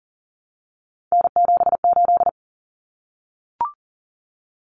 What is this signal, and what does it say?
Morse code, heard as a keyed tone, about 40 wpm, 700 Hz. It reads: N78
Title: Common Cut Numbers - 3-Digit 40wpm